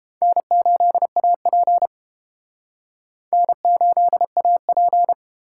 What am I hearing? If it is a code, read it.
Morse code, heard as a keyed tone, about 30 wpm, 700 Hz. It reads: N8AP N8AP